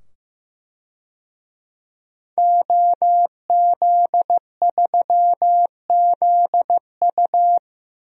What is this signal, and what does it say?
Morse code, heard as a keyed tone, about 16 wpm, 700 Hz. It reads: OZ3ZU
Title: DXCC-100 Call Signs - 5 Characters - 15wpm